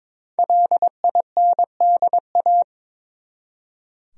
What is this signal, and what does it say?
Morse code, heard as a keyed tone, about 22 wpm, 700 Hz. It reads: LINDA
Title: USA Common Names - Max 5 Letters - 22wpm